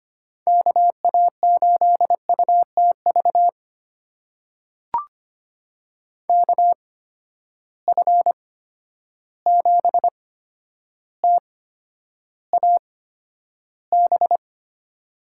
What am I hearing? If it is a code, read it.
Morse code, heard as a keyed tone, about 25 wpm, spaced as for 11 wpm, 700 Hz. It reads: KA8UTV KF7TAB
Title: USA Call Signs - 6 Characters - 8x Character Spacing - 25wpm